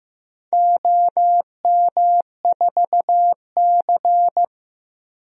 Morse code, heard as a keyed tone, about 15 wpm, 700 Hz. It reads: OM4C
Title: DXCC-100 Call Signs - 4 Characters - Repeated 3x - 15wpm